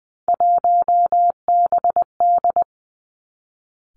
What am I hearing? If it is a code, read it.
Morse code, heard as a keyed tone, about 20 wpm, 700 Hz. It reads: 1BD
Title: Three Challenging Letter-Number Combinations 20wpm